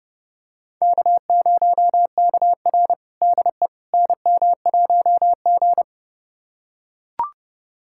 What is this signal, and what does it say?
Morse code, heard as a keyed tone, about 32 wpm, 700 Hz. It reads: K0KR DE NM1G
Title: USA Callsign DE Callsign - 4 Characters - 30wpm